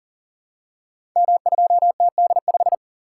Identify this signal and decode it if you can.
Morse code, heard as a keyed tone, about 40 wpm, 700 Hz. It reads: M2TD5
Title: Code Groups - 5 Letter:Numbers - 40wpm